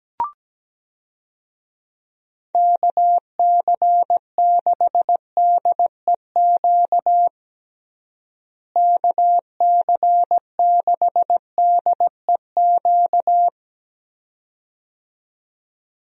Morse code, heard as a keyed tone, about 17 wpm, 700 Hz. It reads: KC6DEQ KC6DEQ
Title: USA Call Signs - 6 Characters - Repeated 2x - 17wpm